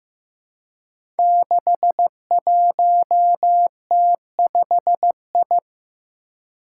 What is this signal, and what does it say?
Morse code, heard as a keyed tone, about 15 wpm, 700 Hz. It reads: 61T5I
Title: Five Challenging Letter-Number Combinations 15wpm